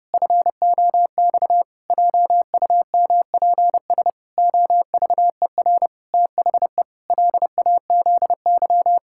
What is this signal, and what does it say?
Morse code, heard as a keyed tone, about 30 wpm, 700 Hz. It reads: FOX JUMPS OVER THE LAZY
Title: Pangram Challenge v2 - 30wpm